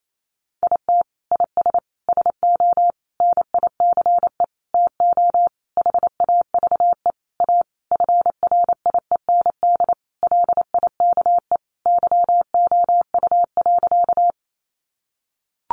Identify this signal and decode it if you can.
Morse code, heard as a keyed tone, about 28 wpm, 700 Hz. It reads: IT IS SO NICE TO HAVE A FRIEND LIKE YOU.
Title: Sentences from Top 1100 Words 28wpm